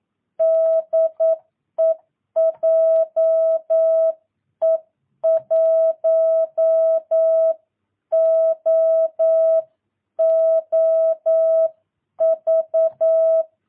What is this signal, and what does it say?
Morse code, heard as a keyed tone, about 9 wpm, 650 Hz. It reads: DE JE1OOV
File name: jh8cbh / 2022cw